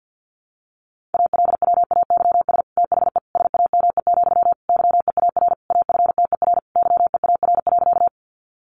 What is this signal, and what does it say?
Morse code, heard as a keyed tone, about 50 wpm, 700 Hz. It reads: ALWAYS THE SAME, YEAR AFTER YEAR.
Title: Sentences from Top 300 Words 50wpm